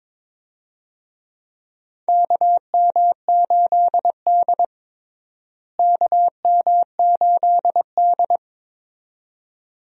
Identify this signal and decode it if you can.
Morse code, heard as a keyed tone, about 21 wpm, 700 Hz. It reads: KM8D KM8D
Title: USA Call Signs - 4 Characters - Repeated 2x - 22wpm